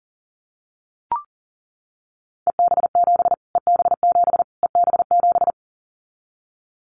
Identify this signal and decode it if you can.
Morse code, heard as a keyed tone, about 40 wpm, 700 Hz. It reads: EB7 EB7 EB7
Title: Callsign Prefixes for Top 25 DXCC - Repeated 3x - 40wpm